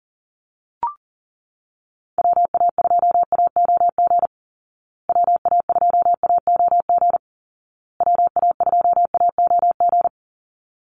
Morse code, heard as a keyed tone, about 40 wpm, 700 Hz. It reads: WA2AOG WA2AOG WA2AOG
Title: Call Signs Repeated 2x - 40wpm